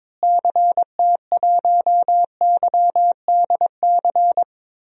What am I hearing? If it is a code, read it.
Morse code, heard as a keyed tone, about 21 wpm, 700 Hz. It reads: CT1YDC
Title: DXCC-100 Call Signs - 6 Characters - Repeated 3x - 22wpm